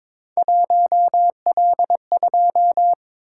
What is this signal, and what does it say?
Morse code, heard as a keyed tone, about 23 wpm, 700 Hz. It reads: 1L2
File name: practice / 22wpm / Three Letter-Number Combinations - 22wpm